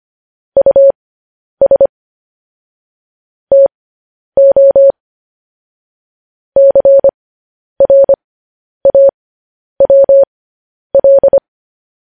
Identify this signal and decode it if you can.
Morse code, heard as a keyed tone, about 25 wpm, spaced as for 10 wpm, 550 Hz. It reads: US TO CRAWL